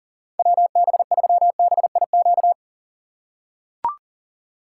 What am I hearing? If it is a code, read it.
Morse code, heard as a keyed tone, about 40 wpm, 700 Hz. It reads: WB3BIQ
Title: K1USN SST QSO Practice - 40wpm